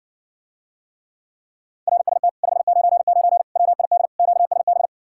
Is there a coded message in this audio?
Yes: Morse code from 1.9 to 4.9 s, heard as a keyed tone, about 60 wpm, 700 Hz. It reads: RST 599 WID QSB